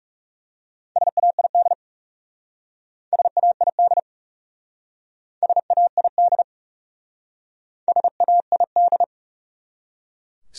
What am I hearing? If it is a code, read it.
Morse code, heard as a keyed tone, about 37 wpm, 700 Hz. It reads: SAID SAID SAID SAID
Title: Speed-Racing Top 100 Words 30wpm